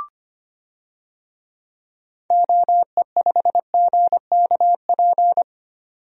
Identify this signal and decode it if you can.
Morse code, heard as a keyed tone, about 25 wpm, 700 Hz. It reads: OE5GKP
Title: DXCC-100 Call Signs - 6 Characters - 25wpm